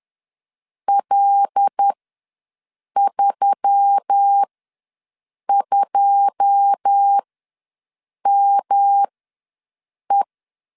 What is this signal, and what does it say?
Morse code, heard as a keyed tone, about 10 wpm, spaced as for 6 wpm, 800 Hz. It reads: L32ME
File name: LEZIONI / 9 28 ws 10wpm 4124